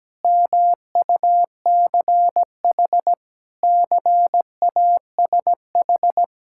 Morse code, heard as a keyed tone, about 17 wpm, 700 Hz. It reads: MUCH CASH
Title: Sentences from Top 1000 Words - Review Part 5 - 17wpm